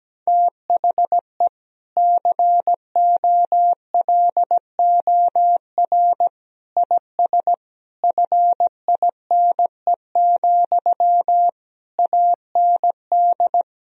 Morse code, heard as a keyed tone, about 17 wpm, 700 Hz. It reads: THE COLOR IS FINE, AND
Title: Sentences from Top 1000 Words - Review Part 2 - 17wpm